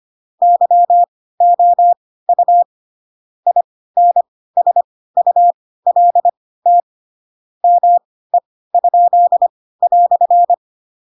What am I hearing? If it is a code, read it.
Morse code, heard as a keyed tone, about 25 wpm, spaced as for 16 wpm, 700 Hz. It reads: YOU INSULT ME?"
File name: Pinocchio / Pinocchio ch2 10 25 ik1hge